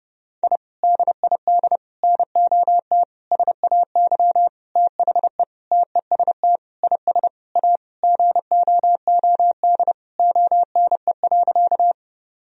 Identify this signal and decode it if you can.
Morse code, heard as a keyed tone, about 32 wpm, 700 Hz. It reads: I DID NOT SAY THE TEST IS A GOOD ONE.
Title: Sentences from Top 1000 Words - Review Part 3 - 30wpm